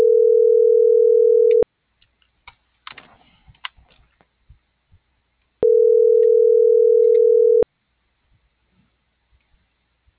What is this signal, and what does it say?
Ringback tone (North America), 440 + 480 Hz.